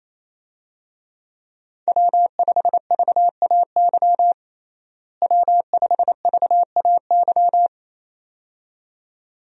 Morse code, heard as a keyed tone, about 28 wpm, 700 Hz. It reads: W5VAY W5VAY
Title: USA Call Signs - 5 Characters - Repeated 2x - 28wpm